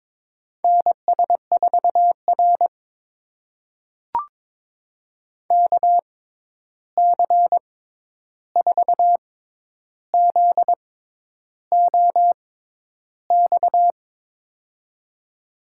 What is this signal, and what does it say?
Morse code, heard as a keyed tone, about 22 wpm, spaced as for 10 wpm, 700 Hz. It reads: NS4R KC4ZOX
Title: USA Call Signs - 6x Character Spacing - 22wpm